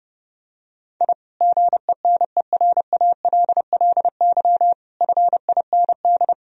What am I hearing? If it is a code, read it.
Morse code, heard as a keyed tone, about 30 wpm, 700 Hz. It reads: I GENERALLY FIND
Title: Sentences from Top 1000 Words - Review Part 4 - 30wpm